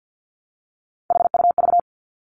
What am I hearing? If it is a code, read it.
Morse code, heard as a keyed tone, about 50 wpm, 700 Hz. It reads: HUV
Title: Three Challenging Letter-Number Combinations 50wpm